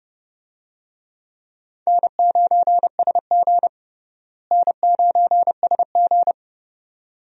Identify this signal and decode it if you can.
Morse code, heard as a keyed tone, about 32 wpm, 700 Hz. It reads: N9SG N9SG
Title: USA Call Signs - 4 Characters - Repeated 2x - 30wpm